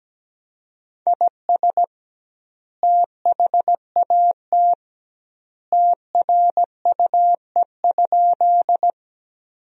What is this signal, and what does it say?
Morse code, heard as a keyed tone, about 17 wpm, 700 Hz. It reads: IS THAT TRUE?
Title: 2x Word-Spacing - Sentences from Top 400 Words - 17wpm